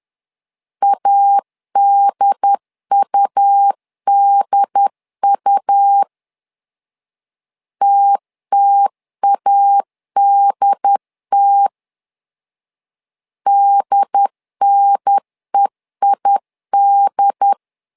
Morse code, heard as a keyed tone, about 10 wpm, 800 Hz. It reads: ADUDU TTADT DNEID
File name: LEZIONI / 2 UHDN / 2 UHDN 10wpm 1904